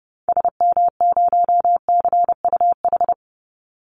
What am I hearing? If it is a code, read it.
Morse code, heard as a keyed tone, about 30 wpm, 700 Hz. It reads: SM0CUH
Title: Call Signs Repeated 3x - 30wpm